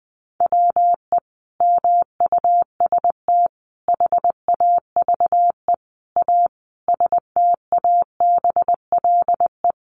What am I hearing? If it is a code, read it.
Morse code, heard as a keyed tone, about 20 wpm, 700 Hz. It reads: WE MUST HAVE A STABLE
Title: Sentences from Top 1500 Words 20wpm